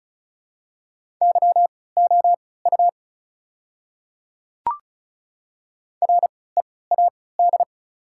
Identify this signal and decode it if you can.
Morse code, heard as a keyed tone, about 36 wpm, spaced as for 20 wpm, 700 Hz. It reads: YOU READ
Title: 3x Character Spacing - Top 200 Words - 35wpm